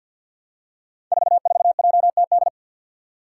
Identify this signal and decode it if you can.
Morse code, heard as a keyed tone, about 47 wpm, 700 Hz. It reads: 44JTD
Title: Five Challenging Letter-Number Combinations 50wpm